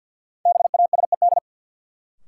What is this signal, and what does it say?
Morse code, heard as a keyed tone, about 51 wpm, 700 Hz. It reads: BASED
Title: Sets of 2 Words - New Word Drill - 50wpm